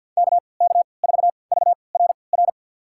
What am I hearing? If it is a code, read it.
Morse code, heard as a keyed tone, about 48 wpm, spaced as for 28 wpm, 700 Hz. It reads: KK4VRR
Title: USA Call Signs - 3x Character Spacing - 50wpm